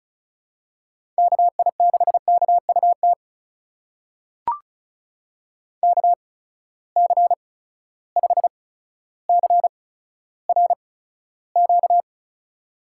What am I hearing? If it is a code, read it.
Morse code, heard as a keyed tone, about 36 wpm, spaced as for 15 wpm, 700 Hz. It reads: KI6KUT KC5CRQ
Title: USA Call Signs - 6 Characters - 8x Character Spacing - 35wpm